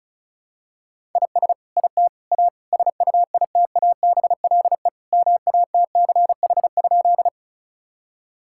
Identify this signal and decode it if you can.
Morse code, heard as a keyed tone, about 34 wpm, 700 Hz. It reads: IS IT A SUITABLE MATCH?
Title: Sentences from Top 1600 Words 35wpm